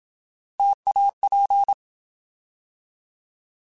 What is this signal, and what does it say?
Morse code, heard as a keyed tone, about 27 wpm, 800 Hz. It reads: TAP
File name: Alphabetical / Alphabetical mp3s / Ts @25